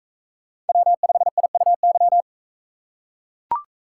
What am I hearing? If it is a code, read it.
Morse code, heard as a keyed tone, about 40 wpm, 700 Hz. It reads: W5IUY